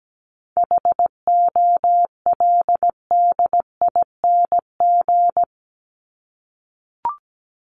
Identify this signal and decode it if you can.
Morse code, heard as a keyed tone, about 17 wpm, 700 Hz. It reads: HOLDING
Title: ING Suffix - Top 100 Words with Base Words 17wpm